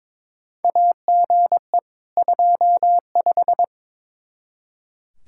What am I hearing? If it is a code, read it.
Morse code, heard as a keyed tone, about 23 wpm, 700 Hz. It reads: AGE 25
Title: Practice QSO - Exchange 3 - More Info - 22wpm